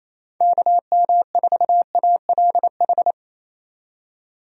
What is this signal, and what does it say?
Morse code, heard as a keyed tone, about 27 wpm, 700 Hz. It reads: KM4ALH
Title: USA Call Signs - 6 Characters - 28wpm